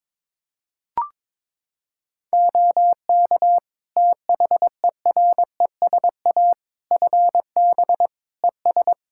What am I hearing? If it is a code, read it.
Morse code, heard as a keyed tone, about 22 wpm, 700 Hz. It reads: OK THERESA FB ES
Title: Practice QSO - Exchange 2 - Info - 22wpm